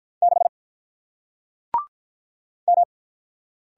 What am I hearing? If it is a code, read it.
Morse code, heard as a keyed tone, about 47 wpm, 700 Hz. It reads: 6 M